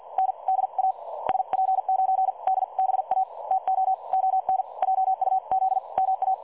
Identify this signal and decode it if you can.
Morse code, heard as a keyed tone, about 50 wpm, spaced as for 42 wpm, 760 Hz. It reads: N DA SP9DBA TO ON JUZ MA
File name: dlazaawansowanych / nagraniazpasma / websdr recording 3565.0kHz